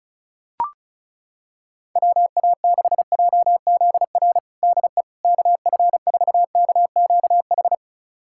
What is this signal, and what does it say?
Morse code, heard as a keyed tone, about 36 wpm, 700 Hz. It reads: WA6JZR DE KF4KQH
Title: USA Callsign DE Callsign - 6 Characters - 35wpm